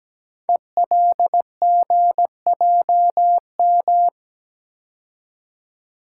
Morse code, heard as a keyed tone, about 17 wpm, 700 Hz. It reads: ELGJM
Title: Code Groups - 5 Letters - 17wpm